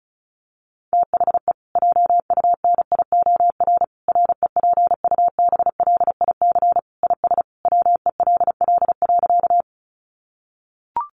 Morse code, heard as a keyed tone, about 34 wpm, 700 Hz. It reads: THE JUNIOR REPUBLIC IS WELL.
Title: Sentences from Top 1500 Words 35wpm